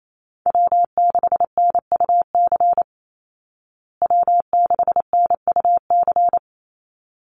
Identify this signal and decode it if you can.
Morse code, heard as a keyed tone, about 28 wpm, 700 Hz. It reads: W6NUC W6NUC
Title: Call Signs Repeated 3x - 28wpm